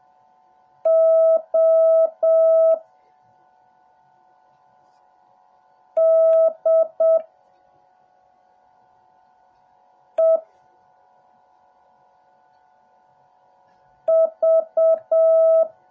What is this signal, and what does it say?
Morse code, heard as a keyed tone, about 7 wpm, 650 Hz. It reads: O D E V
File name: jh8cbh / 2021cw / cw08a